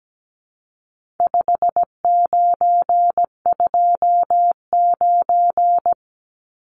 Morse code, heard as a keyed tone, about 17 wpm, 700 Hz. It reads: 5929
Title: Numbers - 4-Digit 17wpm